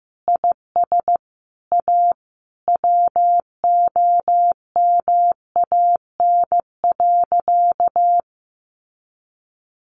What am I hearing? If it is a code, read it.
Morse code, heard as a keyed tone, about 15 wpm, 700 Hz. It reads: IS A WOMAN.
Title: Sentences from Top 1100 Words 15wpm